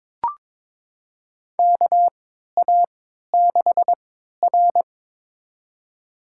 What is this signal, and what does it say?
Morse code, heard as a keyed tone, about 21 wpm, spaced as for 12 wpm, 700 Hz. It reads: KA6R
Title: USA Call Signs - 3x Character Spacing - 22wpm